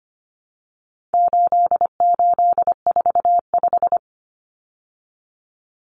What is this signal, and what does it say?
Morse code, heard as a keyed tone, about 25 wpm, 700 Hz. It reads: 8845
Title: Numbers - 4-Digit 25wpm